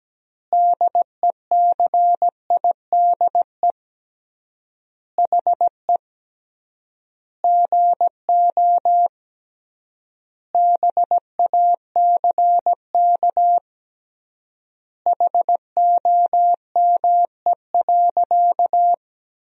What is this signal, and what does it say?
Morse code, heard as a keyed tone, about 17 wpm, 700 Hz. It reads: DECIDE HE GO BACK HOME.
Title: 3x Word-Spacing - Sentences from Top 500 Words - 17wpm